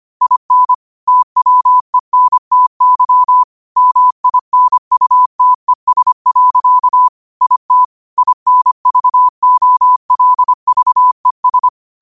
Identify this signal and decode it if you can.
Morse code, heard as a keyed tone, about 25 wpm, 1000 Hz. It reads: IN TWENTY MINUTES. IT INVOLVES
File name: 25wpm